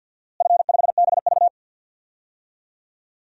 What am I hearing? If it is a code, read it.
Morse code, heard as a keyed tone, about 52 wpm, 700 Hz. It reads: F5BV
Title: DXCC-100 Call Signs - 4 Characters - Repeated 2x - 50wpm